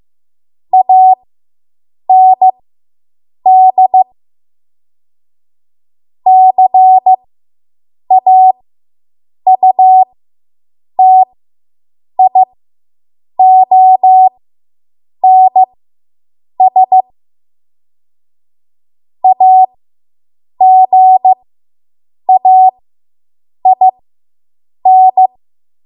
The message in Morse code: AND CAUTIONS AGAIN

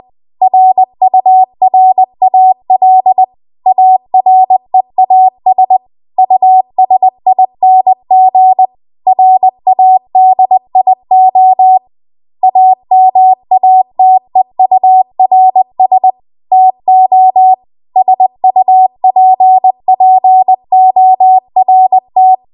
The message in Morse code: RURAL AREAS USING RADIO AMATEURS TO SUPPORT

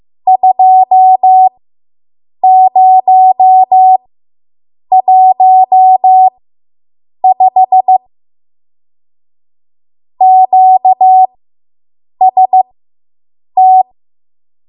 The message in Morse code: 2015 QST

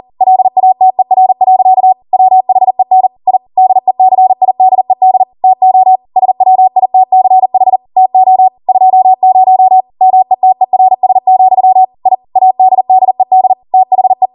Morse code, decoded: LATER. WHEN I DECIDED TO SWITCH TO 20 METERS, I ADDED THE

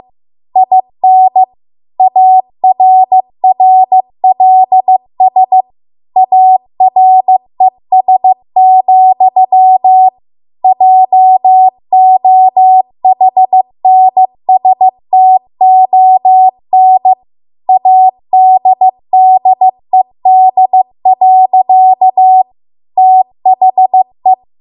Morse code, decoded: IN ARRLS ARES, JOHNSTON ADDED. THE

